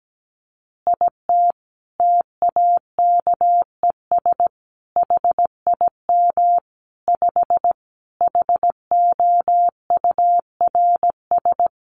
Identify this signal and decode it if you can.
Morse code, heard as a keyed tone, about 17 wpm, 700 Hz. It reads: IT TAKES HIM 5 HOURS
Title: Additional Easy Phrases 17wpm